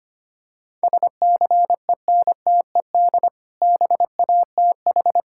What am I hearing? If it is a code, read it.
Morse code, heard as a keyed tone, about 25 wpm, 700 Hz. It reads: SCENTED BATH
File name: practice / 25wpm / Difficult Words with Hints - 25wpm